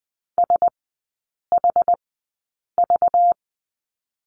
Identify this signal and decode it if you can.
Morse code, heard as a keyed tone, about 20 wpm, spaced as for 8 wpm, 690 Hz. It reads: SHV
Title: 191 Lesson 31e - Letter v - Three Characters 20wpm